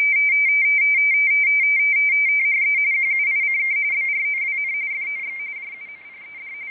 RTTY, about 45.45 baud. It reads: THE QUICK BROWN FOX
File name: rtty / websdr recording 14086.0kHz